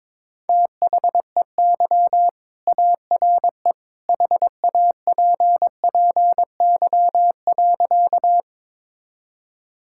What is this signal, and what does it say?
Morse code, heard as a keyed tone, about 22 wpm, 700 Hz. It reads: THEY ARE HAPPY.
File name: practice / 22wpm / Sentences with 2-Word Phrases - 22wpm